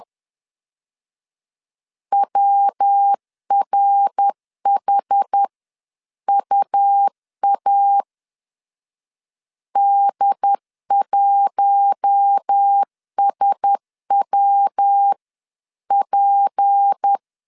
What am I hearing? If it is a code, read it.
Morse code, heard as a keyed tone, about 10 wpm, spaced as for 9 wpm, 800 Hz. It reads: WRHUA D1SWP
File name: CW / DET / 1 6 1500C S B 10wpm 4137